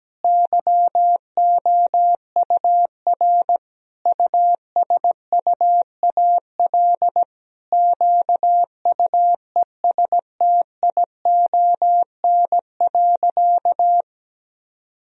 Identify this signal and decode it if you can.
Morse code, heard as a keyed tone, about 17 wpm, 700 Hz. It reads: YOUR USUAL QUESTION.
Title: Sentences from Top 1000 Words - Review Part 2 - 17wpm